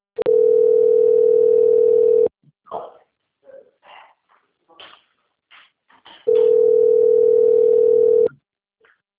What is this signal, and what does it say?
Ringback tone (North America), 440 + 480 Hz.
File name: rg-1600-unknown- - - .29874